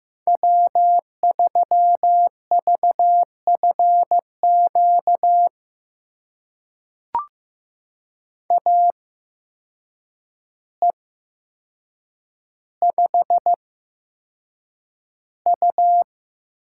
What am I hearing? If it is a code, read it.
Morse code, heard as a keyed tone, about 15 wpm, spaced as for 7 wpm, 700 Hz. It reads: W3VFQ AE5U
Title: USA Call Signs - 5 Characters - 8x Character Spacing - 15wpm